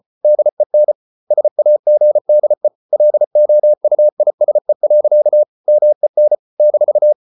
Morse code, heard as a keyed tone, about 34 wpm, 600 Hz. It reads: DEN SAGDE LOUISE. MEN -
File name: Ferie QRQ 34wpm